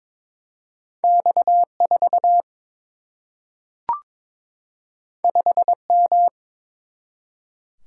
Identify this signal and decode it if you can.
Morse code, heard as a keyed tone, about 22 wpm, 700 Hz. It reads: X 4 5 M
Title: Single-Letter Single-Number 22wpm